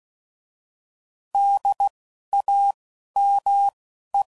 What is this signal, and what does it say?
Morse code, heard as a keyed tone, about 16 wpm, spaced as for 11 wpm, 800 Hz. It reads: DAME
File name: Prefixes / By First Letter / Ds